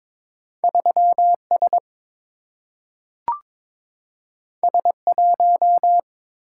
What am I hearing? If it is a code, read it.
Morse code, heard as a keyed tone, about 22 wpm, 700 Hz. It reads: 3 S S 1
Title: Single-Letter Single-Number 22wpm